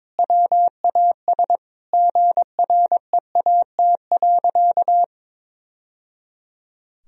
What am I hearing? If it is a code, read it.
Morse code, heard as a keyed tone, about 22 wpm, 700 Hz. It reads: WAS GREAT.
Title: Sentences from Top 1000 Words - Review Part 1 - 22wpm